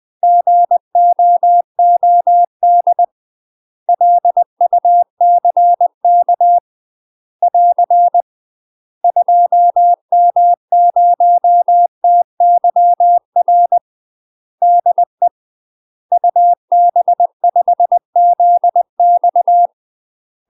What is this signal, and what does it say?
Morse code, heard as a keyed tone, about 20 wpm, 700 Hz. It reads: GOOD LUCK <AR> 2M0TYR DE UB5ZX
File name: UK MASTER QSOs TRAINING / 20WPM / UK MASTER QSOs 0041